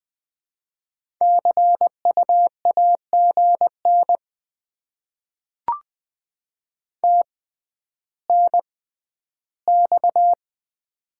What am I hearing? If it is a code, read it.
Morse code, heard as a keyed tone, about 20 wpm, spaced as for 11 wpm, 700 Hz. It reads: CU AGN TNX
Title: Top 20 QSO Elements - 6x Character Spacing - 20wpm